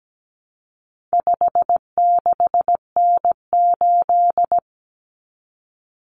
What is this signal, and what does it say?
Morse code, heard as a keyed tone, about 17 wpm, 700 Hz. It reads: 56N8
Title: Common Cut Numbers - 4-Digit 17wpm